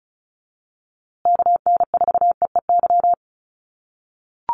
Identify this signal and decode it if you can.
Morse code, heard as a keyed tone, about 34 wpm, 700 Hz. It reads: KN4EEY
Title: USA Call Signs Repeated 3x 35wpm